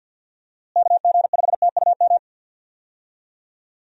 Morse code, heard as a keyed tone, about 51 wpm, 700 Hz. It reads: KG5TUM
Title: USA Call Signs - 6 Characters - Repeated 3x - 50wpm